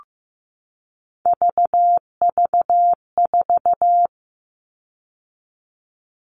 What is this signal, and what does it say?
Morse code, heard as a keyed tone, about 15 wpm, 700 Hz. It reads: VV4
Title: All Cut Numbers - 3-Digit 15wpm